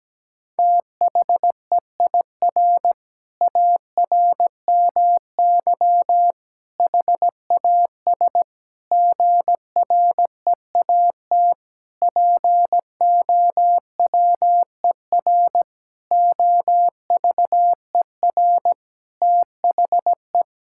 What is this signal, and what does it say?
Morse code, heard as a keyed tone, about 17 wpm, 700 Hz. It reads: THEIR ARMY HAS GREAT POWER OVER THE
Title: Sentences from Top 1000 Words - Review Part 5 - 17wpm